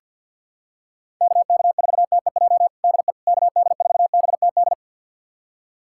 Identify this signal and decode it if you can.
Morse code, heard as a keyed tone, about 47 wpm, 700 Hz. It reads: KK4TEJ DE KD4BTD
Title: USA Callsign DE Callsign - 6 Characters - 50wpm